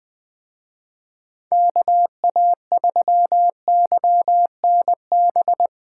Morse code, heard as a keyed tone, about 20 wpm, 700 Hz. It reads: KA3YNB
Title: Speed Racing - USA 2x3 Call Signs 17wpm